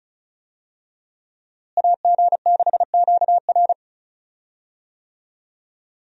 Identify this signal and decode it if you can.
Morse code, heard as a keyed tone, about 34 wpm, 700 Hz. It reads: AG6QR
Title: USA Call Signs - 5 Characters - Repeated 3x - 35wpm